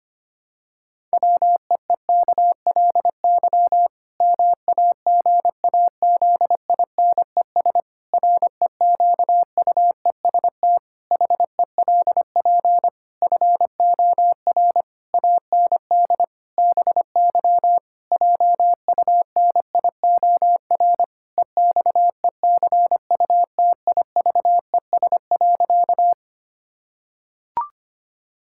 Morse code, heard as a keyed tone, about 25 wpm, 700 Hz. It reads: WEEKLY MAGAZINES REQUEST HELP FOR AND BY JUNIOR EXECUTIVES.
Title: Pangram Challenge v2 - 25wpm